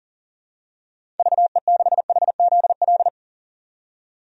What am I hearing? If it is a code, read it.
Morse code, heard as a keyed tone, about 40 wpm, 700 Hz. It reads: VE6HZL
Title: DXCC-100 Call Signs - 6 Characters - 40wpm